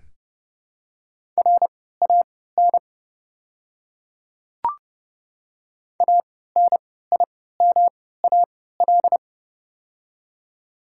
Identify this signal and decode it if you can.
Morse code, heard as a keyed tone, about 31 wpm, spaced as for 17 wpm, 690 Hz. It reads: RAN ANIMAL
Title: 3x Character Spacing - Top 500 Words - 30wpm